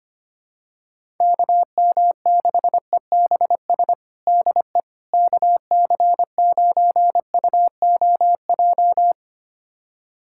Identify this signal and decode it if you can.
Morse code, heard as a keyed tone, about 25 wpm, 700 Hz. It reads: KM6EBS DE KC9UOJ
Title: USA Callsign DE Callsign - 6 Characters - 25wpm